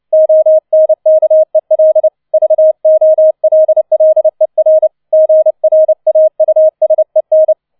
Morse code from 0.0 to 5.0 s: ONKEL VOLLER